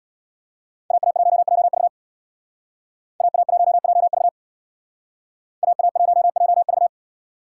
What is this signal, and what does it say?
Morse code, heard as a keyed tone, about 60 wpm, 700 Hz. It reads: AA1JV AA1JV AA1JV
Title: Speed Racing - USA 2x2 Call Signs 50wpm